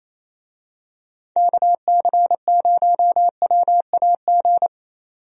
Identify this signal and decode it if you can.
Morse code, heard as a keyed tone, about 27 wpm, 700 Hz. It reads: KC0WAG
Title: USA Call Signs - 6 Characters - 28wpm